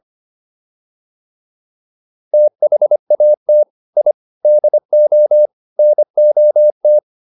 Morse code, heard as a keyed tone, about 25 wpm, 600 Hz. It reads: THAT I DO NOT